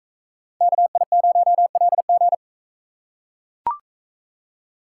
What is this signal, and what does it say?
Morse code, heard as a keyed tone, about 42 wpm, 700 Hz. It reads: KI0LG